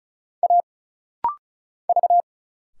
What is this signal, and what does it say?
Morse code, heard as a keyed tone, about 33 wpm, 700 Hz. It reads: A V
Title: Single Letters - Rapid-Fire 35wpm